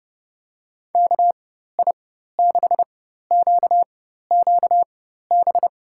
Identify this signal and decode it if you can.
Morse code, heard as a keyed tone, about 31 wpm, spaced as for 14 wpm, 700 Hz. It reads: KI6QQB
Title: USA Call Signs - 6 Characters - 4x Character Spacing - 30wpm